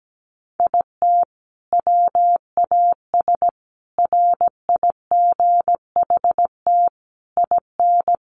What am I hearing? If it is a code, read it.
Morse code, heard as a keyed tone, about 17 wpm, 700 Hz. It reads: IT WAS RIGHT IN
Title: Sentences from Top 500 Words 17wpm